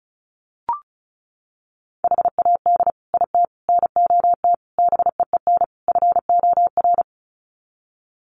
Morse code, heard as a keyed tone, about 34 wpm, 700 Hz. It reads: HAD IT NOT BEEN FOR
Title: Sets of 5 Words Encore2 35wpm